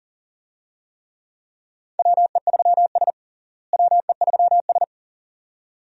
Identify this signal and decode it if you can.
Morse code, heard as a keyed tone, about 37 wpm, 700 Hz. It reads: WE3S WE3S
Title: USA Call Signs - 4 Characters - Repeated 2x - 40wpm